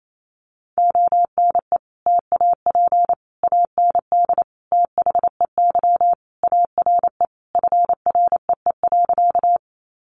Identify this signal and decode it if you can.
Morse code, heard as a keyed tone, about 28 wpm, 700 Hz. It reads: ONE TAP AND THEY ARE FREE.
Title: Sentences from Top 1400 Words 28wpm